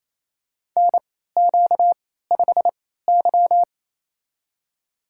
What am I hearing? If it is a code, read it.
Morse code, heard as a keyed tone, about 28 wpm, spaced as for 16 wpm, 700 Hz. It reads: NQ5Y
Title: USA Call Signs - 4 Characters - 3x Character Spacing - 28wpm